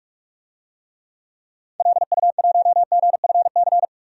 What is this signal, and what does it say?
Morse code, heard as a keyed tone, about 42 wpm, 700 Hz. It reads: LU1GFC
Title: DXCC-100 Call Signs - 6 Characters - 45wpm